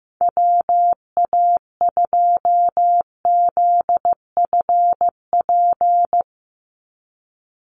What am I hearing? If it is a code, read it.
Morse code, heard as a keyed tone, about 15 wpm, 700 Hz. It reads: WA2ZFP
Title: USA Call Signs 15wpm